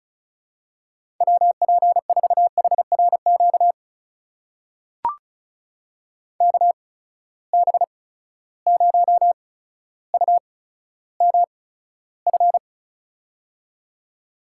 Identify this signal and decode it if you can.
Morse code, heard as a keyed tone, about 36 wpm, spaced as for 15 wpm, 700 Hz. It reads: WP4HRQ KB0UMF
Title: USA Call Signs - 6 Characters - 8x Character Spacing - 35wpm